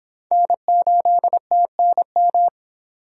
Morse code, heard as a keyed tone, about 25 wpm, 700 Hz. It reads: N8TNM